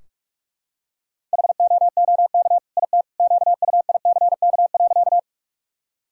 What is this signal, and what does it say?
Morse code, heard as a keyed tone, about 46 wpm, 700 Hz. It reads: HOOK IT QUICK.
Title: Sentences from Top 1600 Words 45wpm